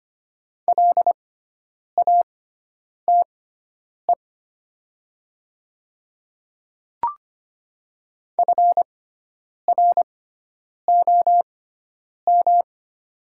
Morse code, heard as a keyed tone, about 25 wpm, spaced as for 9 wpm, 700 Hz. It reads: LATE FROM
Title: 6x Character Spacing - Top 500 Words - 25wpm